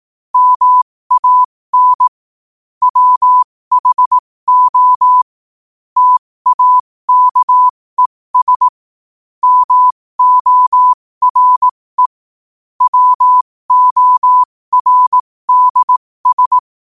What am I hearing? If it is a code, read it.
Morse code, heard as a keyed tone, about 18 wpm, spaced as for 15 wpm, 1000 Hz. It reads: MAN WHO TAKES MORE WORDS
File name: qotdfiles / 15wpm